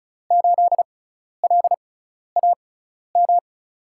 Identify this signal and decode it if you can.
Morse code, heard as a keyed tone, about 36 wpm, spaced as for 12 wpm, 700 Hz. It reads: 8LAM
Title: USA Call Signs - 6 Characters - 6x Character Spacing - 35wpm